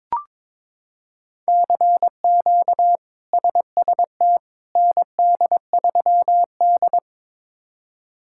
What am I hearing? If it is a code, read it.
Morse code, heard as a keyed tone, about 22 wpm, 700 Hz. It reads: CQ SST ND3D
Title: K1USN SST QSO Practice - 22wpm